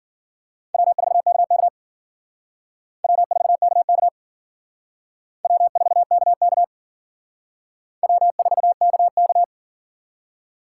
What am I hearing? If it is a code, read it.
Morse code, heard as a keyed tone, about 50 wpm, 700 Hz. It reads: W4KK W4KK W4KK W4KK
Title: Speed Racing - USA 1x2 Call Signs 40wpm